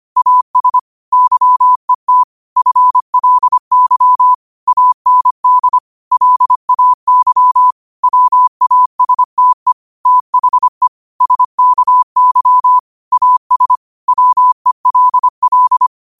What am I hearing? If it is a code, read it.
Morse code, heard as a keyed tone, about 25 wpm, 1000 Hz. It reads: AS YET FLY AND LAY WASTE THE SKY AS WELL
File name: qotdfiles / 25wpm